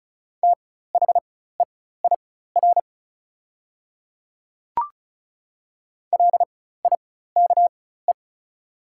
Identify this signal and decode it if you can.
Morse code, heard as a keyed tone, about 36 wpm, spaced as for 16 wpm, 700 Hz. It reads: THEIR LIKE
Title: 4x Character Spacing - Top 100 Words - 35wpm